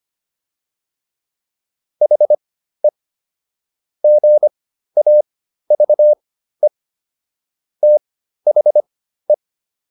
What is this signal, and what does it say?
Morse code, heard as a keyed tone, about 25 wpm, spaced as for 13 wpm, 600 Hz. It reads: HE GAVE THE